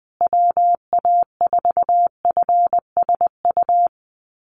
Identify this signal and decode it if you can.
Morse code, heard as a keyed tone, about 20 wpm, 700 Hz. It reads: WA4FSU
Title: USA Call Signs Repeated 2x 20wpm